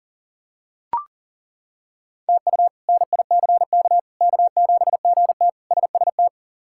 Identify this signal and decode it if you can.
Morse code, heard as a keyed tone, about 40 wpm, 700 Hz. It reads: TU NICK K7GT SST
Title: K1USN SST QSO Practice - 40wpm